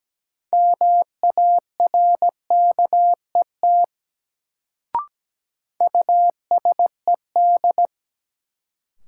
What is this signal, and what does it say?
Morse code, heard as a keyed tone, about 17 wpm, 700 Hz. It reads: MARKET USED
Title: Sentences from Top 600 Words - New Word Drill - 17wpm